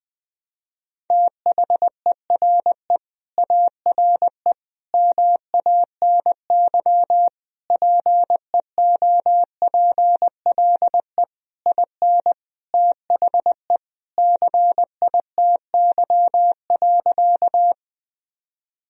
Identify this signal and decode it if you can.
Morse code, heard as a keyed tone, about 20 wpm, 700 Hz. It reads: THERE ARE MANY PEOPLE IN THE CITY.
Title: Sentences from Top 1000 Words - Review Part 1 - 20wpm